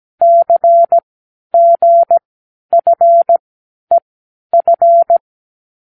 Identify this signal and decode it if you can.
Morse code, heard as a keyed tone, about 17 wpm, spaced as for 10 wpm, 680 Hz. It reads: CGFEF
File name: k7qo / Lesson12(A-H)